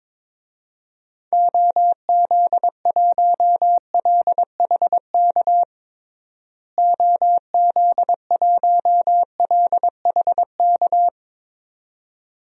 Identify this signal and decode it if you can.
Morse code, heard as a keyed tone, about 22 wpm, 700 Hz. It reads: OZ1LHK OZ1LHK
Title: DXCC-100 Call Signs - 6 Characters - Repeated 2x - 22wpm